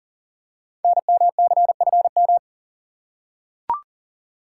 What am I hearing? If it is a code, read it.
Morse code, heard as a keyed tone, about 40 wpm, 690 Hz. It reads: NMCFM
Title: Code Groups - 5 Letters - 40wpm